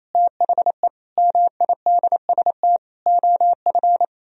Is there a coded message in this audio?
Yes: Morse code from 0.1 to 4.0 s, heard as a keyed tone, about 37 wpm, 700 Hz. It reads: THE MIDST OF